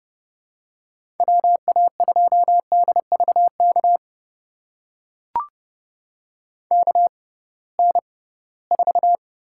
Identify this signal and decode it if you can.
Morse code, heard as a keyed tone, about 30 wpm, spaced as for 19 wpm, 700 Hz. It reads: WA2DVK KN4
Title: USA Call Signs - 6 Characters - 6x Character Spacing - 30wpm